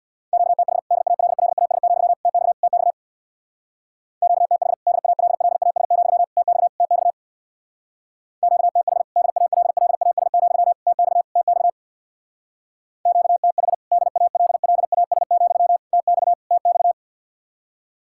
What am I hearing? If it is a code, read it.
Morse code, heard as a keyed tone, about 60 wpm, 700 Hz. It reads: QTH DALLAS, TX TX QTH DALLAS, TX TX QTH DALLAS, TX TX QTH DALLAS, TX TX
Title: Speed-Racing Common QSO Phrases Encore - 50wpm